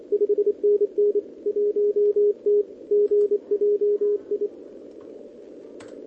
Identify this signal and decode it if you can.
Morse code, heard as a keyed tone, about 26 wpm, 410 Hz. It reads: 5NN 1T GJI
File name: OK QRP 2026 / OKQRP2026